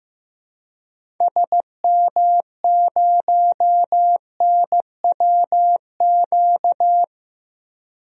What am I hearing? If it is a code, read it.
Morse code, heard as a keyed tone, about 15 wpm, 700 Hz. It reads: SM0NWQ
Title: DXCC-100 Call Signs - 6 Characters - Repeated 3x - 15wpm